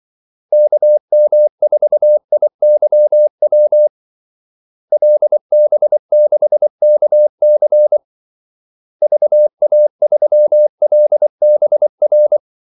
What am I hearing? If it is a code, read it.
Morse code, heard as a keyed tone, about 24 wpm, 600 Hz. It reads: KM4IYW LB6KC VA3LBR
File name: Callsigns-24wpm